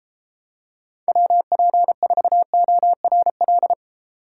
Morse code, heard as a keyed tone, about 33 wpm, 700 Hz. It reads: WP4ORL